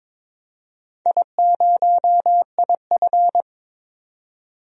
Morse code, heard as a keyed tone, about 22 wpm, 700 Hz. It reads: I0IF
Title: DXCC-100 Call Signs - 4 Characters - 22wpm